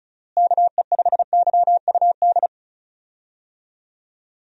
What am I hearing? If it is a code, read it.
Morse code, heard as a keyed tone, about 36 wpm, 700 Hz. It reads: KE5YUD
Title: USA Call Signs - 6 Characters - Repeated 2x - 35wpm